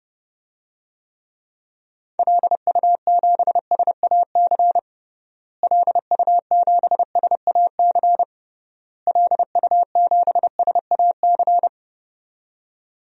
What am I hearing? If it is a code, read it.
Morse code, heard as a keyed tone, about 32 wpm, 700 Hz. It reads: LU7SAC LU7SAC LU7SAC
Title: DXCC-100 Call Signs - 6 Characters - Repeated 3x - 30wpm